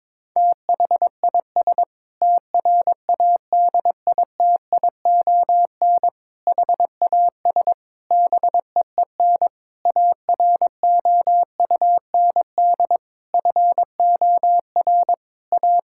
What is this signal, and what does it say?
Morse code, heard as a keyed tone, about 22 wpm, 700 Hz. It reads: THIS TRADITION HAS BEEN AROUND FOR A
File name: Sentences with 7-Word Phrases - 22wpm